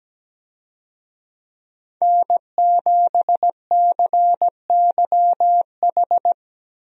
Morse code, heard as a keyed tone, about 17 wpm, 700 Hz. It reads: N7CYH
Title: USA Call Signs - 5 Characters - 17wpm